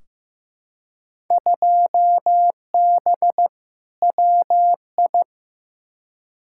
Morse code, heard as a keyed tone, about 15 wpm, 700 Hz. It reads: 2B WI
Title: Field Day Exchanges - 15wpm